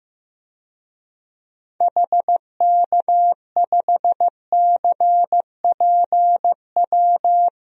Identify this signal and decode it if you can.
Morse code, heard as a keyed tone, about 15 wpm, 700 Hz. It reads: HK5CPW
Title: DXCC-100 Call Signs - 6 Characters - 15wpm